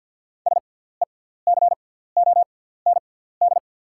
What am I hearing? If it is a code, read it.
Morse code, heard as a keyed tone, about 48 wpm, spaced as for 17 wpm, 690 Hz. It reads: SECOND
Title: 6x Character Spacing - Top 500 Words - 50wpm